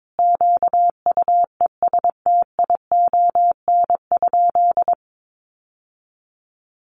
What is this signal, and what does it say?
Morse code, heard as a keyed tone, about 22 wpm, 700 Hz. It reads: QUESTION?
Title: Sentences from Top 600 Words 22wpm